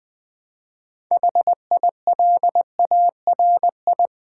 Morse code, heard as a keyed tone, about 20 wpm, 700 Hz. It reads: HILARI
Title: Build A Word - 3 Letters At A Time - 9-Letter Words - 20wpm